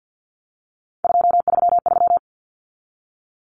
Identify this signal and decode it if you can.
Morse code, heard as a keyed tone, about 50 wpm, 700 Hz. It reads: 233
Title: Numbers - 3-Digit 50wpm